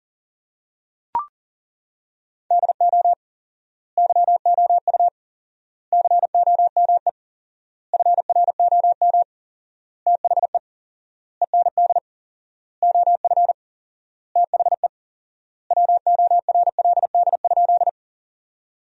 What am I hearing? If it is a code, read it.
Morse code, heard as a keyed tone, about 40 wpm, 700 Hz. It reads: DO YOU COME FROM THE END OF THE WORLD?
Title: 4x Word-Spacing - Sentences from Top 200 Words - 40wpm